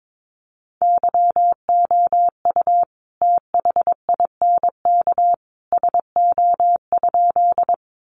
Morse code, heard as a keyed tone, about 22 wpm, 700 Hz. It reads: YOU THINK SO?
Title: Sentences from Top 200 Words 22wpm